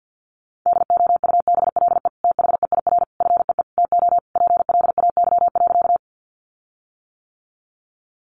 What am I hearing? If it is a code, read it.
Morse code, heard as a keyed tone, about 50 wpm, 700 Hz. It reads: DOUBLE THEIR FEE TO PLAY.
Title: Sentences from Top 1500 Words 50wpm